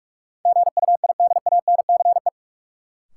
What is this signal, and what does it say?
Morse code, heard as a keyed tone, about 43 wpm, 700 Hz. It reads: GUIDANCE
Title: Sentences from Top 1600 Words - New Word Drill Alt - 45wpm